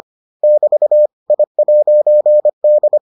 Morse code, heard as a keyed tone, about 25 wpm, 600 Hz. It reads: <BT> I'D